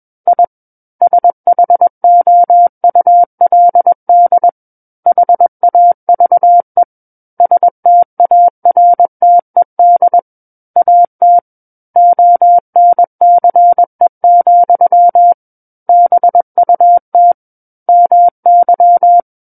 Morse code, heard as a keyed tone, about 21 wpm, 700 Hz. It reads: I SHOULD HAVE STARTED AT ONCE, BUT MY
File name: Book One / 12-Chapter 12 - 21 WPM